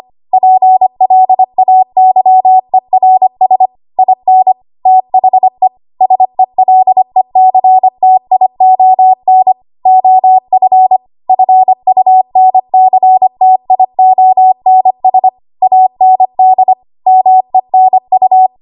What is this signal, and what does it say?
Morse code, heard as a keyed tone, about 25 wpm, 750 Hz. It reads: PLAYERS IN THE SELECTION OF FUNCTIONS AND MENU
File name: Archive / 25 WPM / 25WPM